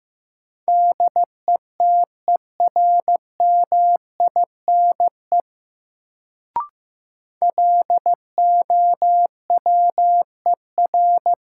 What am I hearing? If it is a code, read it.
Morse code, heard as a keyed tone, about 15 wpm, 700 Hz. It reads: DETERMINE LOWER
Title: Sentences from Top 700 Words - New Word Drill - 15wpm